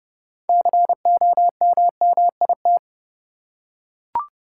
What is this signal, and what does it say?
Morse code, heard as a keyed tone, about 30 wpm, 700 Hz. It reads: COMMIT
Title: Six Letter Words - 30wpm